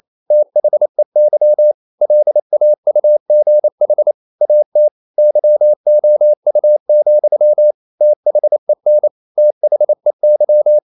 Morse code, heard as a keyed tone, about 28 wpm, 600 Hz. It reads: THEY LAUGH AT YOU, THEN THEY
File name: CW / Practice / EnglishQuotations0228